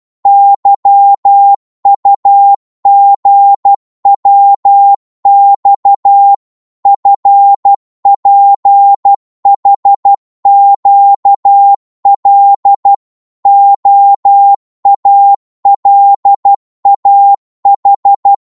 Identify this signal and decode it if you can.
Morse code, heard as a keyed tone, about 12 wpm, 800 Hz. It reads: YUGWX FPHQL OALAH
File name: Morse / ltr-12wpm